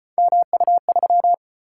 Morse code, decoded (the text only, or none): MU3